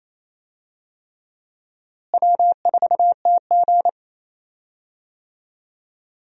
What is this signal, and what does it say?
Morse code, heard as a keyed tone, about 28 wpm, 700 Hz. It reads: W4TG
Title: USA Call Signs - 4 Characters - Repeated 3x - 28wpm